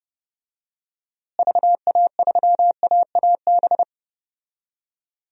Morse code, heard as a keyed tone, about 30 wpm, 700 Hz. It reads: VA3AAB
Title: DXCC-100 Call Signs - 6 Characters - Repeated 3x - 30wpm